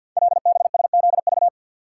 Morse code, heard as a keyed tone, about 48 wpm, 700 Hz. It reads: LBSZV
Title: Code Groups - 5 Letters - 50wpm